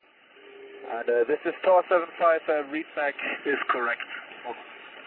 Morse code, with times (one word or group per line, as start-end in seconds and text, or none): none